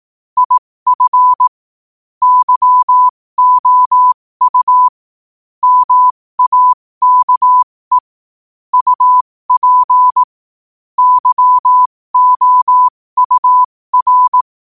Morse code, heard as a keyed tone, about 18 wpm, spaced as for 15 wpm, 1000 Hz. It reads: IF YOU MAKE UP YOUR